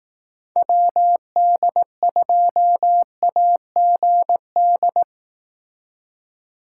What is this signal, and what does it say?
Morse code, heard as a keyed tone, about 18 wpm, 700 Hz. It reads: WD2AGD
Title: Speed Racing - USA Call Signs 15wpm